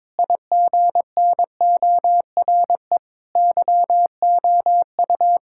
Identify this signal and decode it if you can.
Morse code, heard as a keyed tone, about 22 wpm, 700 Hz. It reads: IGNORE YOU
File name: practice / 22wpm / Sets of 8 Words - 22wpm